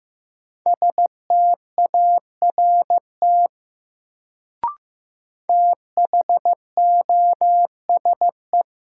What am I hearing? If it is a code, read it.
Morse code, heard as a keyed tone, about 15 wpm, 700 Hz. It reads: START THOSE
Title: Sentences from Top 300 Words - New Word Drill - 15wpm